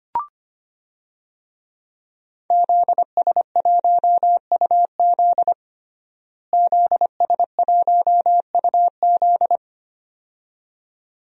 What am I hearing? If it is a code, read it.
Morse code, heard as a keyed tone, about 25 wpm, 700 Hz. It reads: ZS1UZ ZS1UZ
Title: DXCC-100 Call Signs - 5 Characters - Repeated 2x - 25wpm